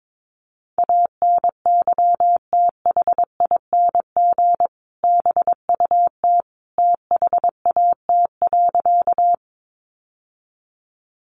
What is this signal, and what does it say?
Morse code, heard as a keyed tone, about 22 wpm, 700 Hz. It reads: ANYTHING BUT THAT.
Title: Sentences from Top 1000 Words 22wpm